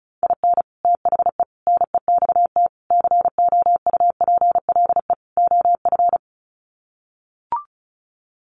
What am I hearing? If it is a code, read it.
Morse code, heard as a keyed tone, about 34 wpm, 700 Hz. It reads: IN THE NEXT COUPLE OF
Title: Sets of 5 Words Encore2 35wpm